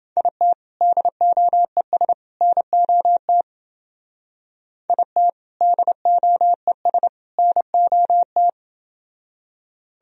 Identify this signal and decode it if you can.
Morse code, heard as a keyed tone, about 28 wpm, 700 Hz. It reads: IT DOES NOT IT DOES NOT